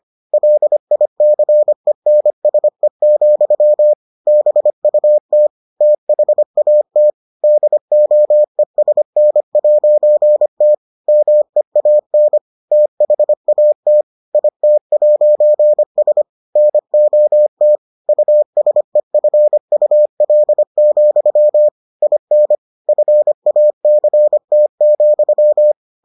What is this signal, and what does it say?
Morse code, heard as a keyed tone, about 25 wpm, 600 Hz. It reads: LICENSE, BUT THAT DOESN'T MEAN THAT IT'S NOT USEFUL, IN FACT,